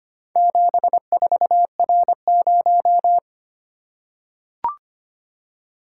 Morse code, heard as a keyed tone, about 25 wpm, 700 Hz. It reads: 74R0
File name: practice / 25wpm / Four Letter-Number Combinations - 25wpm